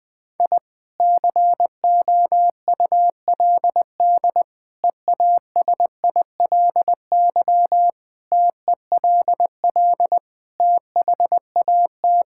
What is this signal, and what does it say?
Morse code, heard as a keyed tone, about 21 wpm, 700 Hz. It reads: I COULD EASILY TELL THAT
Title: Sentences from Top 1000 Words - Review Part 6 - 20wpm